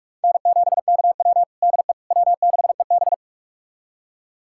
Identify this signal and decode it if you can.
Morse code, heard as a keyed tone, about 46 wpm, 700 Hz. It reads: N7KW DE W6EB
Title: USA Callsign DE Callsign - 4 Characters - 45wpm